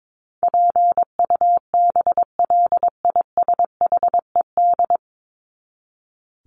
Morse code, heard as a keyed tone, about 22 wpm, 700 Hz. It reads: PUBLISHED
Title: ED Suffix - Top 100 Words with Base Words 22wpm